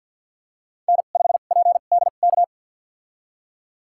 Morse code, heard as a keyed tone, about 52 wpm, spaced as for 36 wpm, 700 Hz. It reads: N5PDK
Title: USA Call Signs - 5 Characters - 2x Character Spacing - 50wpm